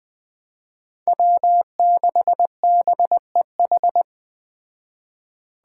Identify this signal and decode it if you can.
Morse code, heard as a keyed tone, about 20 wpm, 700 Hz. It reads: W6BEH
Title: Speed Racing - USA 1x3 Call Signs 17wpm